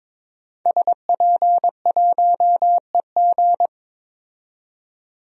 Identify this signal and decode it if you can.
Morse code, heard as a keyed tone, about 22 wpm, 700 Hz. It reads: SP1EG
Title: DXCC-100 Call Signs - 5 Characters - 22wpm